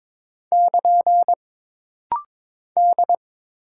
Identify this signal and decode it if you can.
Morse code, heard as a keyed tone, about 21 wpm, 700 Hz. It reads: <KN> D
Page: This rapid-fire timing is geared to push your ICR (Instant Character Recognition) skills.